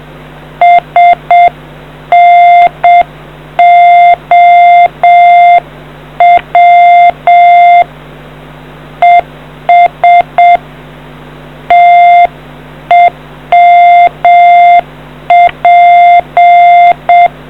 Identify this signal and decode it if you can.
Morse code, heard as a keyed tone, about 7 wpm, 700 Hz. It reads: SNOW ES TEMP